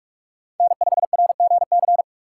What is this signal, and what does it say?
Morse code, heard as a keyed tone, about 43 wpm, 700 Hz. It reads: N5RGC
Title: USA Call Signs - 5 Characters - 2x Character Spacing - 45wpm